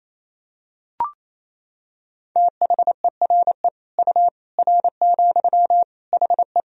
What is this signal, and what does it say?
Morse code, heard as a keyed tone, about 28 wpm, 700 Hz. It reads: THERE U R, HE
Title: Sentences from Top 100 Words and Abbreviations 28wpm